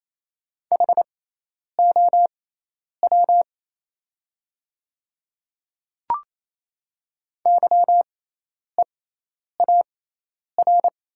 Morse code, heard as a keyed tone, about 28 wpm, spaced as for 10 wpm, 700 Hz. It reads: HOW YEAR
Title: 6x Character Spacing - Top 200 Words - 28wpm